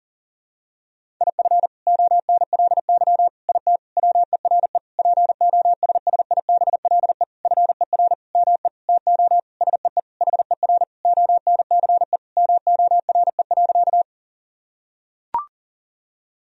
Morse code, heard as a keyed tone, about 40 wpm, 700 Hz. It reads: IF ONLY IT WERE POSSIBLE FER ME TO SEE HER ONCE MORE.
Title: Sentences from Top 500 Words and Abbreviations 40wpm